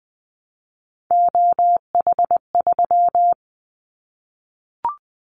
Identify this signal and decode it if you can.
Morse code, heard as a keyed tone, about 20 wpm, 700 Hz. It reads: OH3
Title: Callsign Prefixes for Top 50 DXCC - 20wpm